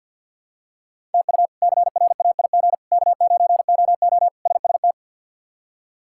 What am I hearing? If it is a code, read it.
Morse code, heard as a keyed tone, about 50 wpm, 700 Hz. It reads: TU CRAIG K9OO SST
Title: K1USN SST QSO Practice - 50wpm